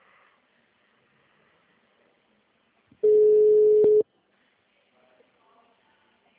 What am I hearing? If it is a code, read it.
Ringback tone (Europe, 425 Hz).